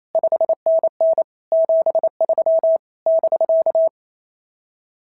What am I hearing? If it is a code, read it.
Morse code, heard as a keyed tone, about 27 wpm, 650 Hz. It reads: 5NN 73 <BK>
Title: SOTA QSO Practice 28wpm